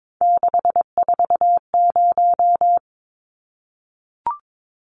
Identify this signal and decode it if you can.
Morse code, heard as a keyed tone, about 22 wpm, 700 Hz. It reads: 640
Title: Numbers - 3-Digit 22wpm